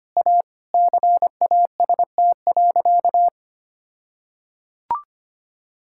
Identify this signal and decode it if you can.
Morse code, heard as a keyed tone, about 25 wpm, 700 Hz. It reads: A CAST.
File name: practice / 25wpm / All-Age Appropriate Jokes - 25wpm